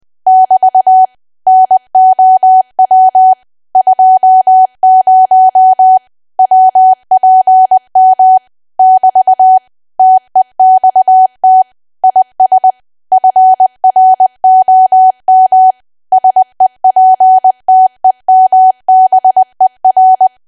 Morse code, heard as a keyed tone, about 20 wpm, 750 Hz. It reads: <BT> NOW 20 WPM <BT> TEXT IS FROM SEPTEMBER
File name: cwa / W1AW-Level3 / 20WPM